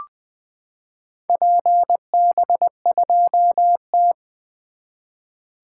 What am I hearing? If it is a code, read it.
Morse code, heard as a keyed tone, about 20 wpm, 700 Hz. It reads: PB2T
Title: 2023 SOTA Activator Call Signs - 20wpm